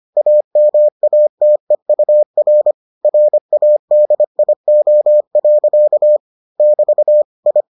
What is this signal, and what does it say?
Morse code, heard as a keyed tone, about 25 wpm, 600 Hz. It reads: AMATEUR RADIO. <BT> I